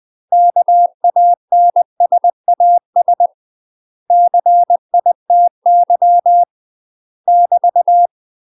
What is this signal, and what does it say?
Morse code, heard as a keyed tone, about 20 wpm, 700 Hz. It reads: KANSAS CITY <BT>
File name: UK MASTER QSOs TRAINING / 20WPM / UK MASTER QSOs 0061